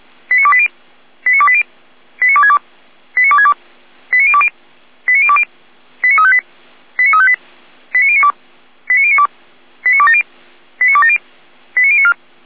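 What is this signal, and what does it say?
Five-tone selective call (ZVEI), heard as quick runs of five tones, tones of about 70 ms. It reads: 89280 89280 89262 89262 89020 89020 89358 89358 89092 89092 89270 89270 89094
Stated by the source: ZVEI five-tone selective call received on an SDR.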